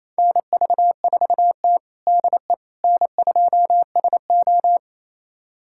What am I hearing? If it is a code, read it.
Morse code, heard as a keyed tone, about 28 wpm, 690 Hz. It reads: NV4T DE N2SO
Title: USA Callsign DE Callsign - 4 Characters - 28wpm